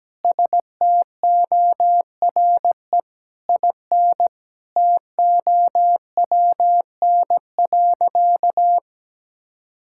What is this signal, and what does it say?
Morse code, heard as a keyed tone, about 17 wpm, 700 Hz. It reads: STORE IN TOWN.
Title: Sentences from Top 1000 Words - Review Part 5 - 17wpm